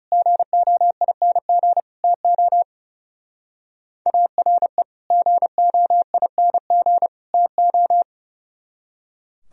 Morse code, heard as a keyed tone, about 37 wpm, 700 Hz. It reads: GOING TO ARE GOING TO
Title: Speed-Racing - Sets of 3 Words 30wpm